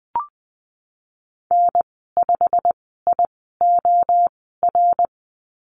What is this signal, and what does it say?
Morse code, heard as a keyed tone, about 20 wpm, spaced as for 15 wpm, 700 Hz. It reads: N5IOR
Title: 173 Lesson 28g - Number 9 - Call Signs 20wpm